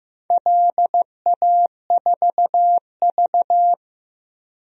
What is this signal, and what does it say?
Morse code, heard as a keyed tone, about 15 wpm, 700 Hz. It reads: LA4V
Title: DXCC-100 Call Signs - 4 Characters - Repeated 2x - 15wpm